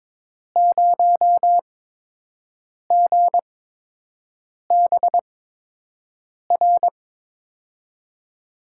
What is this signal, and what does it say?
Morse code, heard as a keyed tone, about 22 wpm, spaced as for 6 wpm, 700 Hz. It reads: 0GBR
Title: USA Call Signs - 5 Characters - 8x Character Spacing - 22wpm